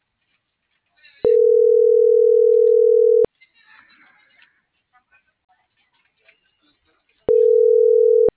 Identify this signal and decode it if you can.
Ringback tone (North America), 440 + 480 Hz.